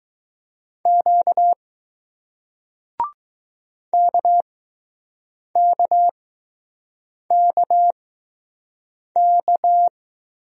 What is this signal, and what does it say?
Morse code, heard as a keyed tone, about 20 wpm, spaced as for 6 wpm, 700 Hz. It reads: Q KKKK